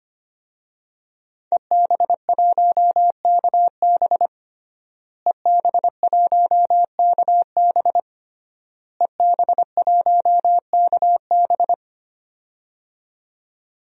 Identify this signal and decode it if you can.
Morse code, heard as a keyed tone, about 25 wpm, 700 Hz. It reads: EB1KB EB1KB EB1KB
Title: DXCC-100 Call Signs - 5 Characters - Repeated 3x - 25wpm